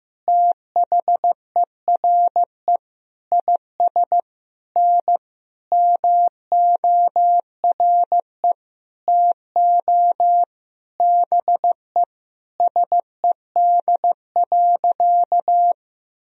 Morse code, heard as a keyed tone, about 15 wpm, 700 Hz. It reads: THERE IS N MORE TO BE SED.
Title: Sentences from Top 100 Words and Abbreviations 15wpm